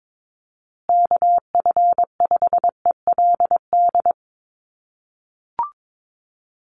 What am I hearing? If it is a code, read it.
Morse code, heard as a keyed tone, about 22 wpm, 700 Hz. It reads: KF5ELD
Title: USA Call Signs Repeated 2x 22wpm